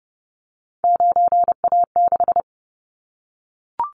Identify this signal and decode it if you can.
Morse code, heard as a keyed tone, about 30 wpm, 700 Hz. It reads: 9A6
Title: Callsign Prefixes for Top 100 DXCC - 30wpm